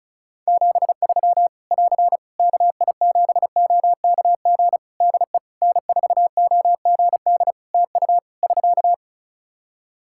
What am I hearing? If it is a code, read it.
Morse code, heard as a keyed tone, about 34 wpm, 700 Hz. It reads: 73 <AR> KI7OKG DE N4OGD TU <SK>
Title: K1USN SST QSO Practice 35wpm